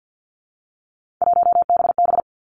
Morse code, heard as a keyed tone, about 50 wpm, 700 Hz. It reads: 1BB
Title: Three Challenging Letter-Number Combinations 50wpm